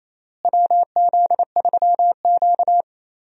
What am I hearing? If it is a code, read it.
Morse code, heard as a keyed tone, about 28 wpm, 700 Hz. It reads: WZ3Q
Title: USA Call Signs - 4 Characters - Repeated 2x - 28wpm